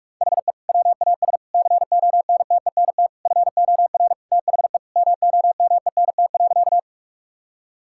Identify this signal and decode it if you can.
Morse code, heard as a keyed tone, about 46 wpm, 700 Hz. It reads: HE WAS CONTENT FOR THE MOMENT.
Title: Sentences from Top 1000 Words - Review Part 6 - 45wpm